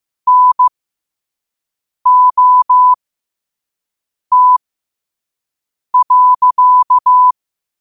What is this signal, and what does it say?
Morse code, heard as a keyed tone, about 15 wpm, spaced as for 5 wpm, 1000 Hz. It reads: NOT.
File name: qotdfiles / 5wpm